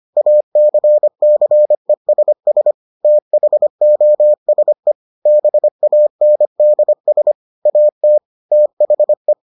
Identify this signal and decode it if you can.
Morse code, heard as a keyed tone, about 25 wpm, 600 Hz. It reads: ACCESS THOSE BANDS AT THE